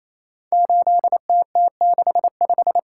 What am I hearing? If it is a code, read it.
Morse code, heard as a keyed tone, about 27 wpm, 700 Hz. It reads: 8TT65
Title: Code Groups - 5 Letter:Numbers - 28wpm